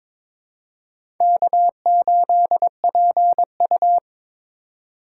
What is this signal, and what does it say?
Morse code, heard as a keyed tone, about 22 wpm, 700 Hz. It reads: K8PU
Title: USA Call Signs - 2x Character Spacing - 22wpm